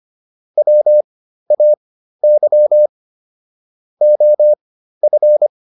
Morse code, heard as a keyed tone, about 25 wpm, spaced as for 13 wpm, 600 Hz. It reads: WAY OF